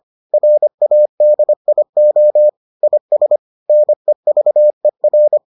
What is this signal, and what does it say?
Morse code, heard as a keyed tone, about 25 wpm, 600 Hz. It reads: RADIO IS NEVER